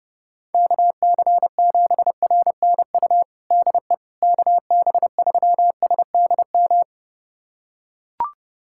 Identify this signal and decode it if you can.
Morse code, heard as a keyed tone, about 30 wpm, 700 Hz. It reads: KC7RNU DE KB3SDM
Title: USA Callsign DE Callsign - 6 Characters - 30wpm